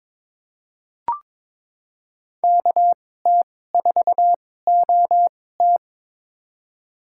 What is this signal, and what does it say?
Morse code, heard as a keyed tone, about 22 wpm, spaced as for 16 wpm, 700 Hz. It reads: KT4OT
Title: USA Call Signs - 5 Characters - 2x Character Spacing - 22wpm